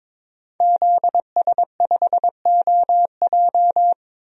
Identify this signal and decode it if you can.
Morse code, heard as a keyed tone, about 22 wpm, 700 Hz. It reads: ZS5OJ
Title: DXCC-100 Call Signs - 5 Characters - 22wpm